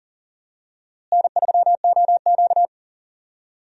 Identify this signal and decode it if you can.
Morse code, heard as a keyed tone, about 37 wpm, 700 Hz. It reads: N3OQ
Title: USA Call Signs - 4 Characters - Repeated 2x - 40wpm